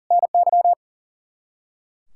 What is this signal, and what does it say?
Morse code, heard as a keyed tone, about 40 wpm, 700 Hz. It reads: NY
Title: K1USN SST QSO Practice 40wpm